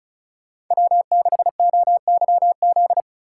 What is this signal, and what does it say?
Morse code, heard as a keyed tone, about 44 wpm, 700 Hz. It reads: W6OYZ
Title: Speed Racing - USA 1x3 Call Signs 35wpm